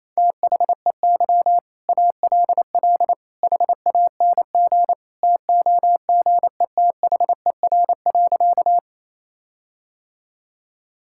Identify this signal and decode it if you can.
Morse code, heard as a keyed tone, about 28 wpm, 700 Hz. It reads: THEY ALL HANG TOGETHER.
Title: Sentences from Top 1600 Words 28wpm